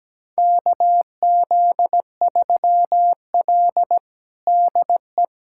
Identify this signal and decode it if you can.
Morse code, heard as a keyed tone, about 17 wpm, 700 Hz. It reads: KZ3L DE
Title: USA Callsign DE Callsign - 4 Characters - 17wpm